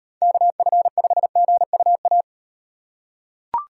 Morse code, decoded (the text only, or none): KF5GUA